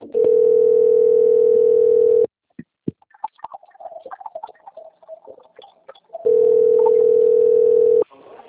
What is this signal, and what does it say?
Ringback tone (North America), 440 + 480 Hz.